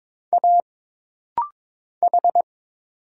Morse code, heard as a keyed tone, about 22 wpm, 700 Hz. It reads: A H
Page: This rapid-fire timing is geared to push your ICR (Instant Character Recognition) skills.